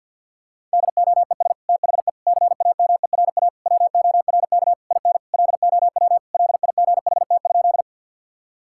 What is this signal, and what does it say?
Morse code, heard as a keyed tone, about 50 wpm, 700 Hz. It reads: DOES THE CAMERA WORK IN LOW LIGHT?
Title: Sentences from Top 1000 Words - Review Part 5 - 50wpm